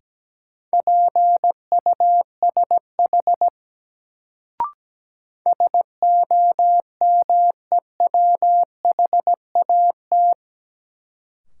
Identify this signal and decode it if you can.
Morse code, heard as a keyed tone, about 17 wpm, 700 Hz. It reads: PUSH SOMEWHAT
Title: Sentences from Top 1300 Words - New Word Drill Alt - 17wpm